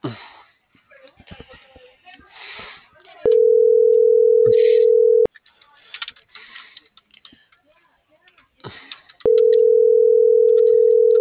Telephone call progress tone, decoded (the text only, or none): ringback tone